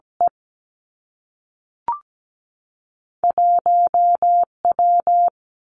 Morse code, heard as a keyed tone, about 17 wpm, spaced as for 3 wpm, 700 Hz. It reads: E1 W
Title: Two Challenging Letter-Number Combinations 17wpm